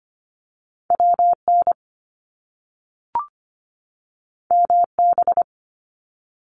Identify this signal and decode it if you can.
Morse code, heard as a keyed tone, about 25 wpm, 700 Hz. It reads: WN MB
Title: Top Two Letter Combinations in Words 25wpm